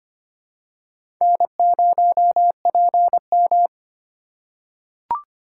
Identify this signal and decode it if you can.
Morse code, heard as a keyed tone, about 25 wpm, 700 Hz. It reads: N0PM
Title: USA Call Signs - 4 Characters - 3x Character Spacing - 25wpm